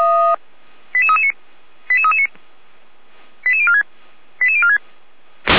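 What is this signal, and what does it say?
Five-tone selective call (ZVEI), heard as quick runs of five tones, tones of about 70 ms. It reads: 88339 88339 88046 88046
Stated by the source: ZVEI five-tone selective call received on an SDR, from feuerwehr-orlamuende (fire brigade Orlamünde).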